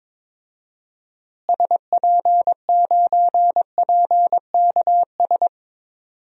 Morse code, heard as a keyed tone, about 22 wpm, 700 Hz. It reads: SP9PKS
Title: DXCC-100 Call Signs - 6 Characters - Repeated 2x - 22wpm